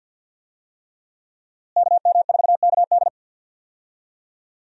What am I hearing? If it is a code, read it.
Morse code, heard as a keyed tone, about 47 wpm, 700 Hz. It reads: KM4KD
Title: USA Call Signs - 5 Characters - Repeated 3x - 50wpm